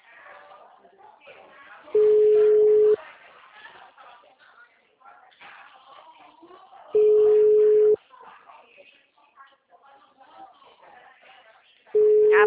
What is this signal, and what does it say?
Ringback tone (Europe, 425 Hz), with speech.